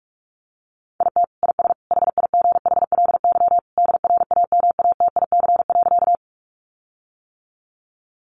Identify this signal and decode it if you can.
Morse code, heard as a keyed tone, about 44 wpm, 700 Hz. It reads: IT IS HIGHLY DRAMATIC.
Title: Sentences from Top 1300 Words 45wpm